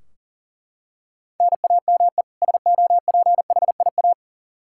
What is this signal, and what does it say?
Morse code, heard as a keyed tone, about 37 wpm, 700 Hz. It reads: NAME SOPHIA
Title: Practice QSO - Exchange 1 - Beginning - 40wpm